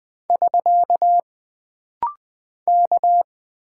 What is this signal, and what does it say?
Morse code, heard as a keyed tone, about 20 wpm, 700 Hz. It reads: <SK> K
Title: Single Character (Letter:Number:Punctuation:Prosign) - Rapid-Fire - 20wpm